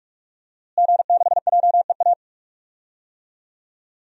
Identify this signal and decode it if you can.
Morse code, heard as a keyed tone, about 45 wpm, 700 Hz. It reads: G6JEA
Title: DXCC-100 Call Signs - 5 Characters - Repeated 3x - 45wpm